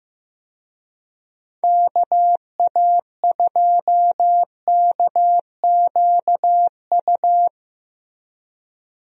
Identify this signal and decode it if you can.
Morse code, heard as a keyed tone, about 15 wpm, 700 Hz. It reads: KA2KQU
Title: USA Call Signs - 6 Characters - Repeated 2x - 15wpm